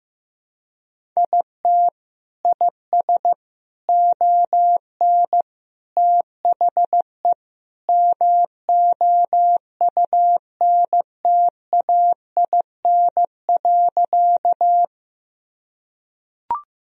Morse code, heard as a keyed tone, about 15 wpm, 700 Hz. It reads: IT IS ON THE MOUNTAIN.
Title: Sentences from Top 1000 Words - Review Part 1 - 15wpm